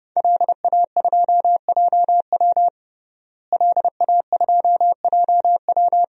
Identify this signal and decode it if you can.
Morse code, heard as a keyed tone, about 32 wpm, 700 Hz. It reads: LA2JW LA2JW
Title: DXCC-100 Call Signs - 5 Characters - Repeated 2x - 30wpm